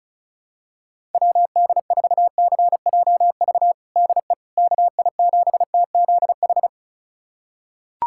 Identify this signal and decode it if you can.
Morse code, heard as a keyed tone, about 34 wpm, 700 Hz. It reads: WD4CJV DE KI7TZH
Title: USA Callsign DE Callsign - 6 Characters - 35wpm